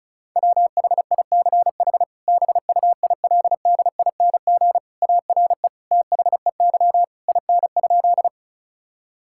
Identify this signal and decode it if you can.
Morse code, heard as a keyed tone, about 34 wpm, 700 Hz. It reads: WHICH BUILDING ARE THEY IN?
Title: Sentences from Top 1000 Words - Review Part 5 - 35wpm